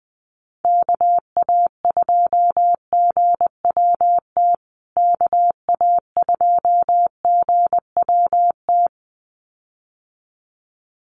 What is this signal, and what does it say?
Morse code, heard as a keyed tone, about 20 wpm, 700 Hz. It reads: KA2GWT KA2GWT
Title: USA Call Signs Repeated 2x 20wpm